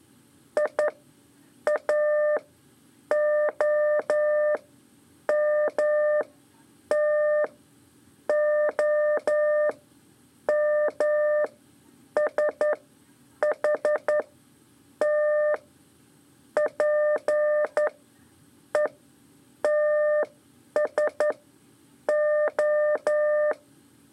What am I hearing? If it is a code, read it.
Morse code, heard as a keyed tone, about 11 wpm, spaced as for 7 wpm, 600 Hz. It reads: IAOMTOMSHT PETSO
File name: 3 lezione CW 2 AIR